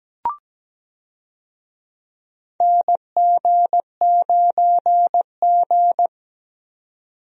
Morse code, heard as a keyed tone, about 17 wpm, 700 Hz. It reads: NG9G
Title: USA Call Signs - 4 Characters - 17wpm